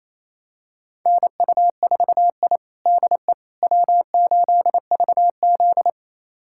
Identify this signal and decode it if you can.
Morse code, heard as a keyed tone, about 28 wpm, 700 Hz. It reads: NU4I DE W8VZ
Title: USA Callsign DE Callsign - 4 Characters - 28wpm